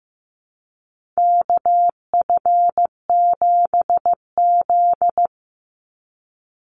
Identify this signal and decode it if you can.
Morse code, heard as a keyed tone, about 15 wpm, 700 Hz. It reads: KF7Z
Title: Call Signs Repeated 3x - 15wpm